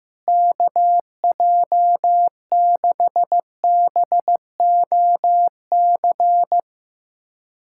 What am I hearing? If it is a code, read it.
Morse code, heard as a keyed tone, about 15 wpm, 700 Hz. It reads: KJ6BOC
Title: USA Callsign DE Callsign - 6 Characters - 15wpm